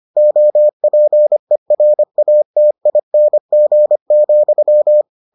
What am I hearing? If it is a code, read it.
Morse code, heard as a keyed tone, about 25 wpm, 600 Hz. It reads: OPERATING,